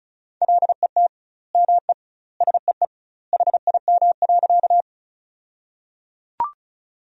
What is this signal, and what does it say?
Morse code, heard as a keyed tone, about 34 wpm, 700 Hz. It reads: LET ME SEE HIM.
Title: 2x Word-Spacing - Sentences from Top 300 Words - 35wpm